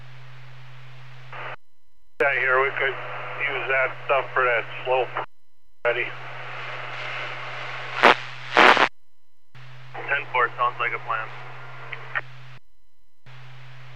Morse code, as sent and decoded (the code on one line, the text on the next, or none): none